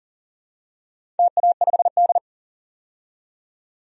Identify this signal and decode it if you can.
Morse code, heard as a keyed tone, about 38 wpm, 700 Hz. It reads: TA5D
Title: DXCC-100 Call Signs - 4 Characters - 40wpm